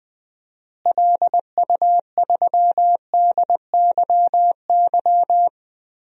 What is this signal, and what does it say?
Morse code, heard as a keyed tone, about 21 wpm, 700 Hz. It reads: LU3DYY
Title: DXCC-100 Call Signs - 6 Characters - Repeated 2x - 20wpm